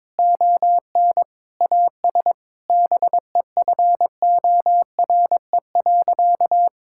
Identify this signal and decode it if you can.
Morse code, heard as a keyed tone, about 21 wpm, 700 Hz. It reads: ON AS BEFORE.
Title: Sentences from Top 1000 Words - Review Part 6 - 22wpm